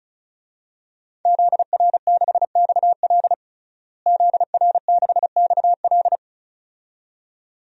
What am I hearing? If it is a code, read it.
Morse code, heard as a keyed tone, about 34 wpm, 700 Hz. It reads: ZR6XL ZR6XL
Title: DXCC-100 Call Signs - 5 Characters - Repeated 2x - 35wpm